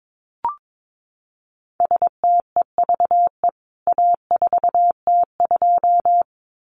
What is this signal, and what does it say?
Morse code, heard as a keyed tone, about 22 wpm, 700 Hz. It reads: STEVE A4T2
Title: CWT Exchange 22wpm